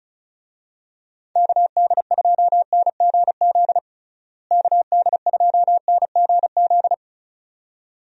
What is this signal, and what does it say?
Morse code, heard as a keyed tone, about 34 wpm, 700 Hz. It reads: KD2NGZ KD2NGZ
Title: USA Call Signs - 6 Characters - Repeated 2x - 35wpm